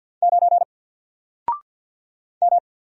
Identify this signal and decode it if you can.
Morse code, heard as a keyed tone, about 48 wpm, 700 Hz. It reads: 9 M
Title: Single Character (Letter:Number:Punctuation:Prosign) Alt 50wpm